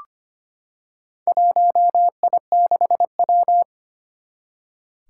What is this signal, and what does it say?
Morse code, heard as a keyed tone, about 25 wpm, 700 Hz. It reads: 1I6W
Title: Four Challenging Letter-Number Combinations 25wpm